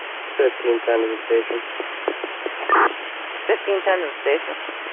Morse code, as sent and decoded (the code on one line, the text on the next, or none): none